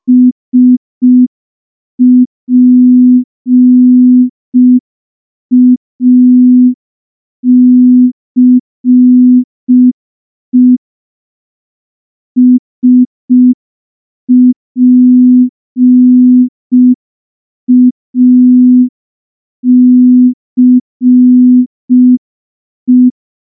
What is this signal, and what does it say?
Morse code, heard as a keyed tone, about 5 wpm, 250 Hz. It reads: SPACE SPACE
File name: morse / morse-training-05WPM